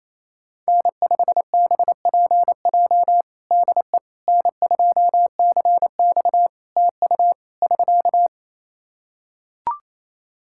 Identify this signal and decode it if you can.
Morse code, heard as a keyed tone, about 28 wpm, 700 Hz. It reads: N5BPJ DE N2CX TU <SK>
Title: K1USN SST QSO Practice 28wpm